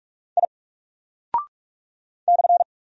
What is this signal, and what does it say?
Morse code, heard as a keyed tone, about 46 wpm, 700 Hz. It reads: I /
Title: Single Character (Letter:Number:Punctuation:Prosign) 45wpm